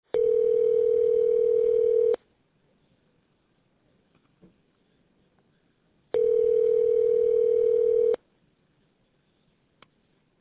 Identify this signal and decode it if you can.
Ringback tone (North America), 440 + 480 Hz.